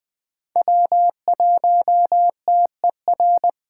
Morse code, heard as a keyed tone, about 20 wpm, 700 Hz. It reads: W1TER
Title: K1USN SST QSO Practice - 20wpm